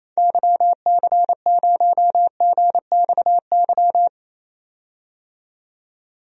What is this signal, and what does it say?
Morse code, heard as a keyed tone, about 28 wpm, 700 Hz. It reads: YC0GXY
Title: DXCC-100 Call Signs - 6 Characters - 28wpm